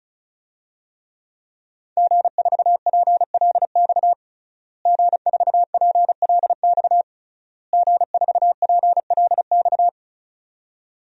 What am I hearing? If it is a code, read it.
Morse code, heard as a keyed tone, about 36 wpm, 700 Hz. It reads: G4PLX G4PLX G4PLX
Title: DXCC-100 Call Signs - 5 Characters - Repeated 3x - 35wpm